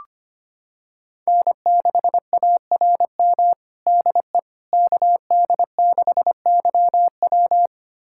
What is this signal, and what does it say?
Morse code, heard as a keyed tone, about 25 wpm, 700 Hz. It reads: N6ARM DE KD6YW
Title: USA Callsign DE Callsign - 5 Characters - 25wpm